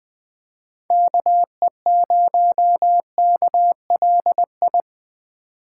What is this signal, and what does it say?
Morse code, heard as a keyed tone, about 21 wpm, 700 Hz. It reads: KE0KLI